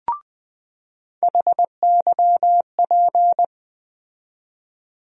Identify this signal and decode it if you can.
Morse code, heard as a keyed tone, about 20 wpm, 700 Hz. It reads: HYP
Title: USA Call Sign Postfixes - 3 Letters 20wpm